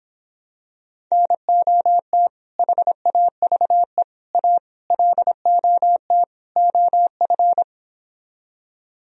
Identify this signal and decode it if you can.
Morse code, heard as a keyed tone, about 26 wpm, 700 Hz. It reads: NOT HAVE A LOT OF